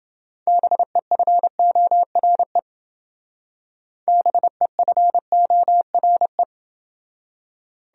Morse code, heard as a keyed tone, about 28 wpm, 700 Hz. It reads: BEFORE BEFORE